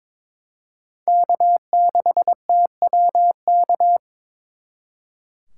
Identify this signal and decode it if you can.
Morse code, heard as a keyed tone, about 26 wpm, 700 Hz. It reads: K6TWK
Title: Speed Racing - USA 1x3 Call Signs 22wpm